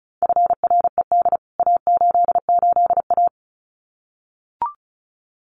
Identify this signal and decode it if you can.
Morse code, heard as a keyed tone, about 34 wpm, 700 Hz. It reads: FRED A88A
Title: CWT Exchange 35wpm